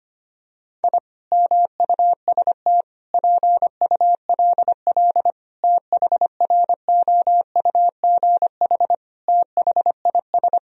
Morse code, heard as a keyed tone, about 25 wpm, 700 Hz. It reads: I MUST PULL THROUGH THIS
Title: Sentences from Top 1000 Words - Review Part 2 - 25wpm